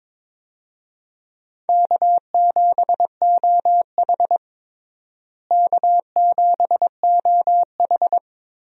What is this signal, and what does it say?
Morse code, heard as a keyed tone, about 22 wpm, 700 Hz. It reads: K7OH K7OH
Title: USA Call Signs - 4 Characters - Repeated 3x - 22wpm